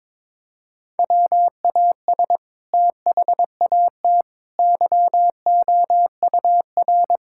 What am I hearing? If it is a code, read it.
Morse code, heard as a keyed tone, about 22 wpm, 700 Hz. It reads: WAS THAT YOUR
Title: Sentences from Top 1600 Words 22wpm